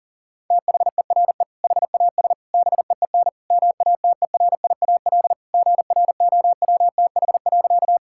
Morse code, heard as a keyed tone, about 40 wpm, 700 Hz. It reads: THERE HAS BEEN MATERIAL GROWTH.
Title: Sentences from Top 1000 Words - Review Part 5 - 40wpm